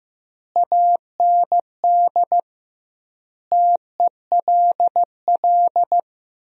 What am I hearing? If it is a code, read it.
Morse code, heard as a keyed tone, about 15 wpm, 700 Hz. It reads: AND TELL
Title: 2x Word-Spacing - Sentences from Top 200 Words - 15wpm